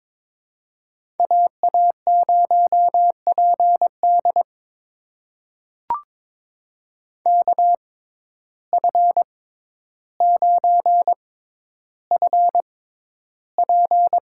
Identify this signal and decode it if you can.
Morse code, heard as a keyed tone, about 22 wpm, spaced as for 11 wpm, 700 Hz. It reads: AA0PD KF9FP
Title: USA Call Signs - 5 Characters - 6x Character Spacing - 22wpm